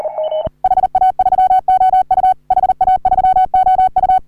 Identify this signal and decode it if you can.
Morse code, heard as a keyed tone, about 39 wpm, 750 Hz. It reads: HA3OU HA3OU
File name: CW / HA3OU RHOBCW REC